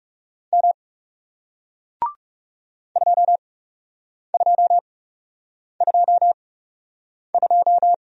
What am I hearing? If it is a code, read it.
Morse code, heard as a keyed tone, about 40 wpm, spaced as for 8 wpm, 700 Hz. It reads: M 2222